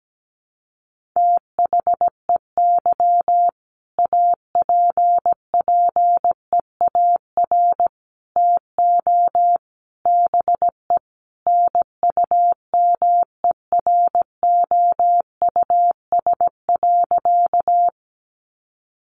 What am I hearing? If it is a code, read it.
Morse code, heard as a keyed tone, about 17 wpm, 700 Hz. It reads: THEY APPEAR TO BE NUMEROUS.
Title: Sentences from Top 1400 Words 17wpm